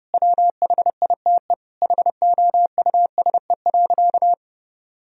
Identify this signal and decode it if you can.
Morse code, heard as a keyed tone, about 30 wpm, 700 Hz. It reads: WHITE HOUSE.
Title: Sentences from Top 1000 Words - Review Part 1 - 30wpm